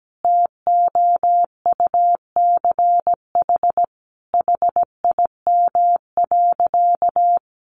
Morse code, heard as a keyed tone, about 17 wpm, 700 Hz. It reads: TOUCH HIM.
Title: Sentences from Top 1200 Words 17wpm